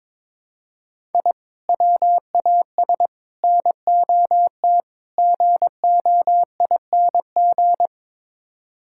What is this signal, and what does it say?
Morse code, heard as a keyed tone, about 21 wpm, 700 Hz. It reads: I WAS NOT GOING
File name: practice / 22wpm / Build A Phrase - 5-Word Phrases - 22wpm